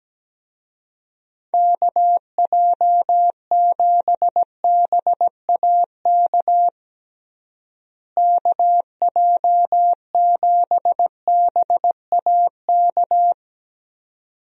Morse code, heard as a keyed tone, about 17 wpm, 700 Hz. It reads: KJ7BAK KJ7BAK
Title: USA Call Signs - 6 Characters - Repeated 2x - 17wpm